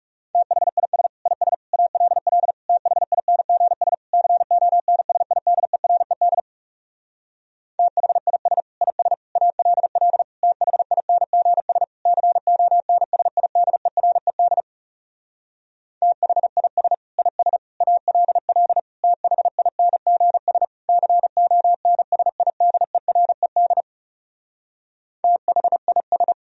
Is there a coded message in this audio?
Yes: Morse code from 0.3 to 26.3 s, heard as a keyed tone, about 40 wpm, 700 Hz. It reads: THIS IS ALL THINGS CONSIDERED THIS IS ALL THINGS CONSIDERED THIS IS ALL THINGS CONSIDERED THIS